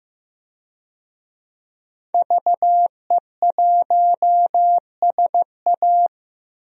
Morse code, heard as a keyed tone, about 15 wpm, 700 Hz. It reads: VE1SA
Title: DXCC-100 Call Signs - 5 Characters - 15wpm